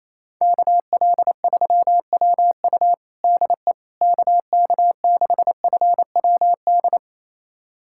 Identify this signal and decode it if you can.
Morse code, heard as a keyed tone, about 27 wpm, 700 Hz. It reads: KL3WU DE KK6FWD
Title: USA Callsign DE Callsign 28wpm